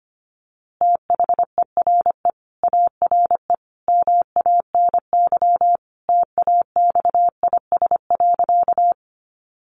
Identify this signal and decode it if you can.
Morse code, heard as a keyed tone, about 25 wpm, 700 Hz. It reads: THERE ARE MANY TAXIS.
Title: Easy Phrases 25wpm